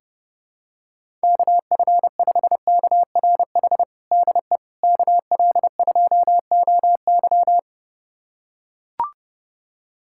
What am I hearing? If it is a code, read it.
Morse code, heard as a keyed tone, about 30 wpm, 700 Hz. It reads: KF5KRH DE KL2OY
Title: USA Callsign DE Callsign 30wpm